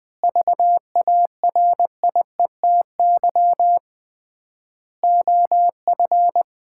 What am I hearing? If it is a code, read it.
Morse code, heard as a keyed tone, about 21 wpm, 700 Hz. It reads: VARIETY OF
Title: 3x Word-Spacing - Sets of 4 Words - 20wpm